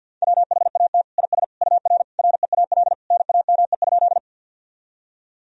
Morse code, heard as a keyed tone, about 47 wpm, 700 Hz. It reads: WHAT IS UR REAL NAME?
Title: Sentences from Top 300 Words and Abbreviations 50wpm